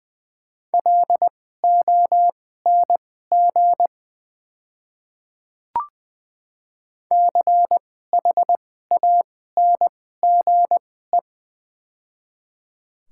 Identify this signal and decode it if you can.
Morse code, heard as a keyed tone, about 20 wpm, spaced as for 14 wpm, 700 Hz. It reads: LONG CHANGE
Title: 2x Character Spacing - Top 300 Words - 20wpm